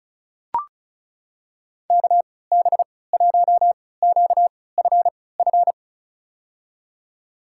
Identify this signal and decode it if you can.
Morse code, heard as a keyed tone, about 36 wpm, spaced as for 20 wpm, 700 Hz. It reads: KB1QFF
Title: USA Call Signs - 6 Characters - 3x Character Spacing - 35wpm